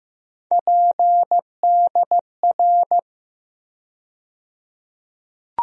Morse code, heard as a keyed tone, about 15 wpm, 700 Hz. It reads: PDR
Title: USA Call Sign Postfixes - 3 Letters 15wpm